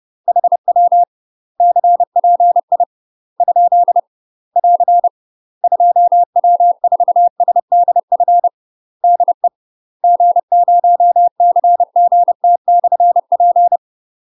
Morse code, heard as a keyed tone, about 30 wpm, 700 Hz. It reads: HW CPI ? <AR> 2W4SDF DE G0CGT/P
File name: UK MASTER QSOs TRAINING / 30WPM / UK MASTER QSOs 0048